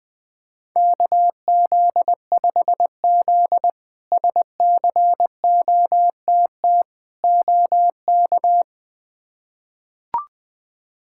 Morse code, heard as a keyed tone, about 20 wpm, 700 Hz. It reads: KZ5Z SCOTT OK
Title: K1USN SST QSO Practice - 20wpm